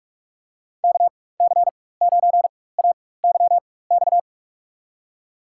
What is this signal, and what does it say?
Morse code, heard as a keyed tone, about 46 wpm, spaced as for 21 wpm, 700 Hz. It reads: KC9AYX
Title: USA Call Signs - 6 Characters - 4x Character Spacing - 45wpm